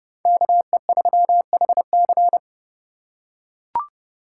Morse code, heard as a keyed tone, about 27 wpm, 700 Hz. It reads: KE3HC